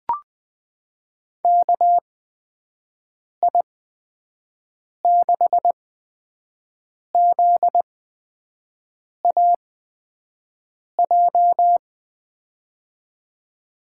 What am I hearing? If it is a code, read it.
Morse code, heard as a keyed tone, about 20 wpm, spaced as for 5 wpm, 700 Hz. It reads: KI6ZAJ
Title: USA Call Signs - 6 Characters - 8x Character Spacing - 20wpm